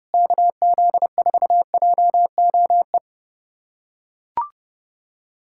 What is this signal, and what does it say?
Morse code, heard as a keyed tone, about 30 wpm, 700 Hz. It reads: KZ4JOE
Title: USA Callsign DE Callsign - 6 Characters - 30wpm